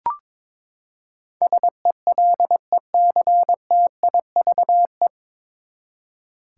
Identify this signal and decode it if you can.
Morse code, heard as a keyed tone, about 22 wpm, 700 Hz. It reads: SELECTIVE
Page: Morse Code Ninja practice 22wpm